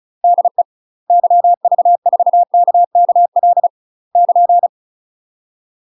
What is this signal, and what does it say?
Morse code, heard as a keyed tone, about 34 wpm, 700 Hz. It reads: DE YV4KKL <KN>
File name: UK MASTER QSOs TRAINING / 35WPM / UK MASTER QSOs 0007